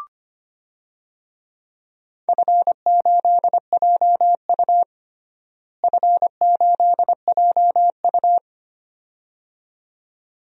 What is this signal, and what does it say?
Morse code, heard as a keyed tone, about 25 wpm, 700 Hz. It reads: F8JU F8JU
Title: DXCC-100 Call Signs - 4 Characters - Repeated 2x - 25wpm